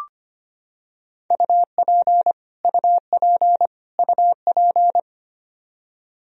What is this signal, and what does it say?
Morse code, heard as a keyed tone, about 25 wpm, 700 Hz. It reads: UP UP UP
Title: Top 100 Words Repeated 25wpm